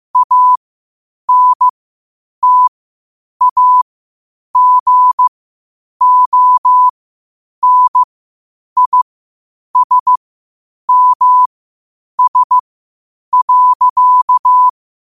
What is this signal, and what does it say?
Morse code, heard as a keyed tone, about 15 wpm, spaced as for 8 wpm, 1000 Hz. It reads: ANTAGONISMS.
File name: qotdfiles / 8wpm